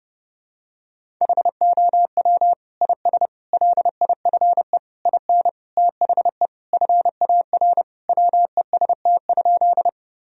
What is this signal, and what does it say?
Morse code, heard as a keyed tone, about 30 wpm, 700 Hz. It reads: HOW IS LIFE IN THE FAR WEST?
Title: Sentences from Top 1000 Words - Review Part 2 - 30wpm